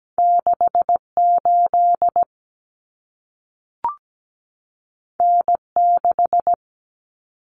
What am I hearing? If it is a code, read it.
Morse code, heard as a keyed tone, about 17 wpm, 700 Hz. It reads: 68 N6
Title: Common Cut Numbers - 2-Digit 17wpm